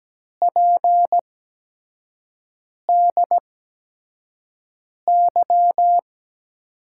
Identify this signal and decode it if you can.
Morse code, heard as a keyed tone, about 17 wpm, spaced as for 5 wpm, 700 Hz. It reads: PDY
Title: USA Call Signs - 5 Characters - 8x Character Spacing - 17wpm